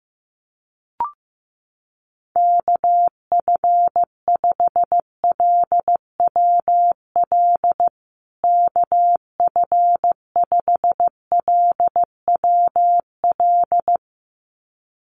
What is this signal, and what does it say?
Morse code, heard as a keyed tone, about 15 wpm, 700 Hz. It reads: KF5LWL KF5LWL
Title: USA Call Signs Repeated 2x 15wpm